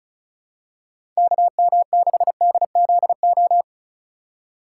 Morse code, heard as a keyed tone, about 36 wpm, 700 Hz. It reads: KM6DZO
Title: USA Call Signs - 6 Characters - 2x Character Spacing - 35wpm